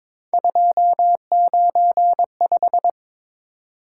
Morse code, heard as a keyed tone, about 22 wpm, 700 Hz. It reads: 295
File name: practice / 22wpm / Three Letter-Number Combinations - 22wpm